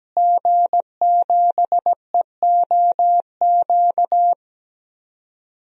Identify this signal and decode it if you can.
Morse code, heard as a keyed tone, about 17 wpm, 700 Hz. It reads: G7EOQ
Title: DXCC-100 Call Signs - 5 Characters - Repeated 2x - 17wpm